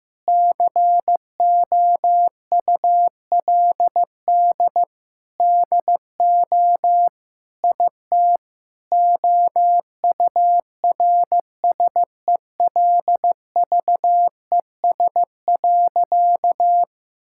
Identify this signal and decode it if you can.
Morse code, heard as a keyed tone, about 15 wpm, 700 Hz. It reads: COULD DO IT OURSELVES.
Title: Sentences from Top 1600 Words 15wpm